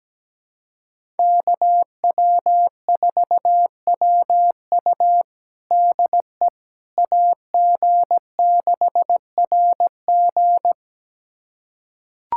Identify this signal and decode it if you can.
Morse code, heard as a keyed tone, about 17 wpm, 700 Hz. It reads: KW4WU DE AG6RG
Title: USA Callsign DE Callsign - 5 Characters - 17wpm